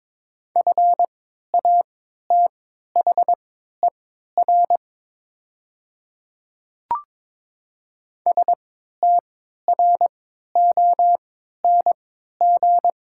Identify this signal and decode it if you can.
Morse code, heard as a keyed tone, about 22 wpm, spaced as for 12 wpm, 690 Hz. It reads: FATHER STRONG
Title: 3x Character Spacing - Top 500 Words - 22wpm